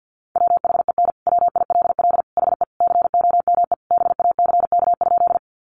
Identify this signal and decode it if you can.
Morse code, heard as a keyed tone, about 48 wpm, 700 Hz. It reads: WHEN WILL HE COME BACK?
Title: Sentences from Top 200 Words 50wpm